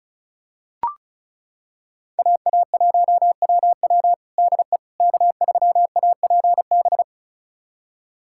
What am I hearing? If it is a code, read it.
Morse code, heard as a keyed tone, about 34 wpm, 700 Hz. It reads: AA1WW DE K3APB
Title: USA Callsign DE Callsign - 5 Characters - 35wpm